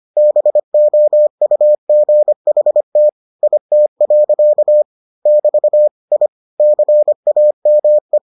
Morse code, heard as a keyed tone, about 25 wpm, 600 Hz. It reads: BOUGHT IT. <BT> I CAME